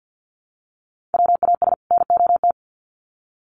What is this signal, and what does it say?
Morse code, heard as a keyed tone, about 50 wpm, 700 Hz. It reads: WAS NOT
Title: Sets of 2 Words 50wpm